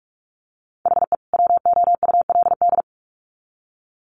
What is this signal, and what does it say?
Morse code, heard as a keyed tone, about 44 wpm, 700 Hz. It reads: HE WOULD
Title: Sets of 2 Words 45wpm